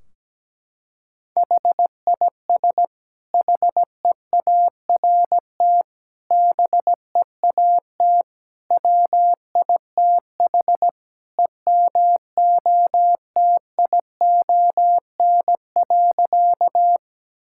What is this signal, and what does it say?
Morse code, heard as a keyed tone, about 17 wpm, 700 Hz. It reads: HIS HEART BEAT WITH EMOTION.
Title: Sentences from Top 1600 Words 17wpm